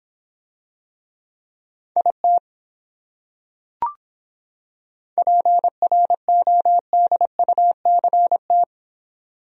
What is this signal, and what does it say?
Morse code, heard as a keyed tone, about 26 wpm, 700 Hz. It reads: IT PRODUCT